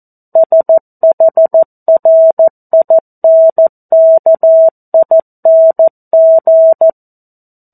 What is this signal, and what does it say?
Morse code, heard as a keyed tone, about 14 wpm, 650 Hz. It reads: SHRINKING
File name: 02-Chapter-02 14 WPM